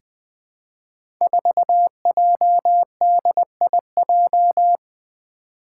Morse code, heard as a keyed tone, about 20 wpm, 700 Hz. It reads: 4JDIJ
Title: Five Challenging Letter-Number Combinations 20wpm